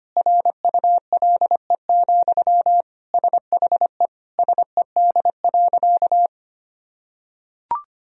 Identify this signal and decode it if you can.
Morse code, heard as a keyed tone, about 25 wpm, 700 Hz. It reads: RULE, SHE SED.
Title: Sentences from Top 400 Words and Abbreviations 25wpm